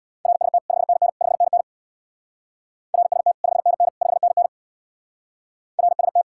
Morse code, heard as a keyed tone, about 65 wpm, 700 Hz. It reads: RST 5NN 5NN RST 5NN 5NN RST